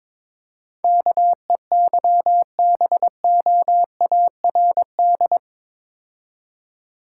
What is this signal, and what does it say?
Morse code, heard as a keyed tone, about 27 wpm, 700 Hz. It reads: KEYBOARD
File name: Easy Words - Speed-Racing with Hints - 22wpm